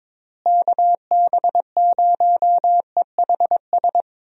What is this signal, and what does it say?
Morse code, heard as a keyed tone, about 22 wpm, 700 Hz. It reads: KB0EHS
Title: USA Call Signs - 6 Characters - Repeated 2x - 22wpm